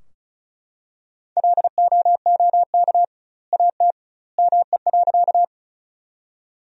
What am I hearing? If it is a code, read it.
Morse code, heard as a keyed tone, about 36 wpm, 700 Hz. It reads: LOOK AT ME.
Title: 2x Word-Spacing - Sentences from Top 200 Words - 35wpm